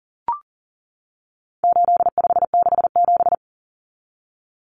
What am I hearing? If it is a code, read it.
Morse code, heard as a keyed tone, about 40 wpm, 700 Hz. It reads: 8567
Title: Numbers - 4-Digit 40wpm